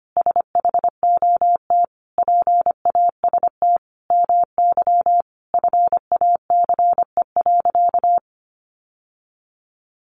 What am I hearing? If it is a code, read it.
Morse code, heard as a keyed tone, about 25 wpm, 700 Hz. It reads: SHOT PAST MY FACE.
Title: Sentences from Top 1100 Words 25wpm